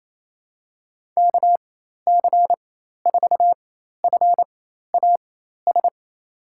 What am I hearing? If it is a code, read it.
Morse code, heard as a keyed tone, about 28 wpm, spaced as for 13 wpm, 700 Hz. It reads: KC4FAS
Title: USA Call Signs - 4x Character Spacing - 28wpm